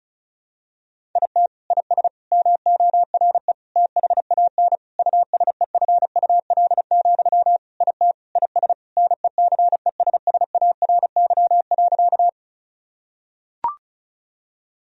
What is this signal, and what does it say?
Morse code, heard as a keyed tone, about 34 wpm, 700 Hz. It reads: IT IS MORE THAN USEFUL, IT IS NECESSARY.
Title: Sentences from Top 1000 Words - Review Part 4 - 35wpm